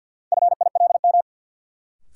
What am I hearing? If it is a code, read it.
Morse code, heard as a keyed tone, about 48 wpm, 700 Hz. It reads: FILM
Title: Sentences from Top 1000 Words - Review Part 3 - New Word Drill - 50wpm